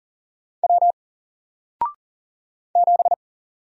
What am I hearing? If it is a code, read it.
Morse code, heard as a keyed tone, about 39 wpm, 700 Hz. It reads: W 7
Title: Single Letter-Number 40wpm